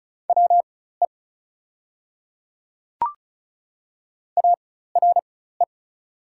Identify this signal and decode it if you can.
Morse code, heard as a keyed tone, about 33 wpm, spaced as for 16 wpm, 690 Hz. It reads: WE ARE
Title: 4x Character Spacing - Top 100 Words - 35wpm